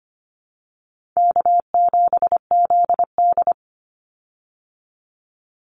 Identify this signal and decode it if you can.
Morse code, heard as a keyed tone, about 25 wpm, 700 Hz. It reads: K7ZD
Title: Easy Call Signs Repeated 3x - 25wpm